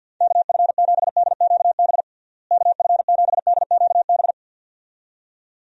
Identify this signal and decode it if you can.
Morse code, heard as a keyed tone, about 46 wpm, 700 Hz. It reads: KF7DQB KF7DQB
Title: USA Call Signs - 6 Characters - Repeated 2x - 50wpm